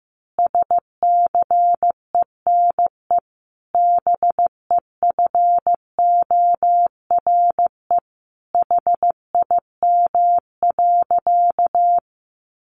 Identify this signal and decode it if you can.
Morse code, heard as a keyed tone, about 15 wpm, 700 Hz. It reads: SCENE BEFORE HIM.
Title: Sentences from Top 1100 Words 15wpm